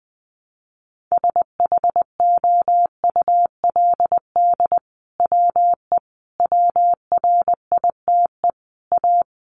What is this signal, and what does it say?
Morse code, heard as a keyed tone, about 20 wpm, 700 Hz. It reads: SHOULD WE WRITE A
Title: Sentences from Top 1300 Words 20wpm